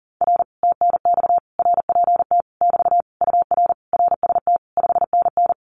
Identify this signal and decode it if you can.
Morse code, heard as a keyed tone, about 40 wpm, 700 Hz. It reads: R TNX RPT <BT> UR RST 5NN
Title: Common QSO Phrases Encore 40wpm